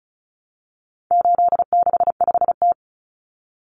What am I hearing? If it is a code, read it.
Morse code, heard as a keyed tone, about 34 wpm, 700 Hz. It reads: 865T
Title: Common Cut Numbers - 4-Digit 35wpm